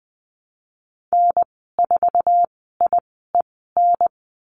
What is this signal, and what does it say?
Morse code, heard as a keyed tone, about 20 wpm, spaced as for 14 wpm, 700 Hz. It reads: N4IEN
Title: 062 Lesson 11g - Letter h - Call Signs 20wpm